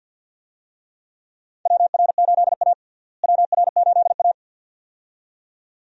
Morse code, heard as a keyed tone, about 52 wpm, 700 Hz. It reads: WR8A WR8A
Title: USA Call Signs - 4 Characters - Repeated 2x - 50wpm